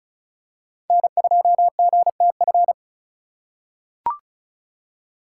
Morse code, heard as a keyed tone, about 35 wpm, 700 Hz. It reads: N2GTF
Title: USA Call Signs - 5 Characters - 3x Character Spacing - 35wpm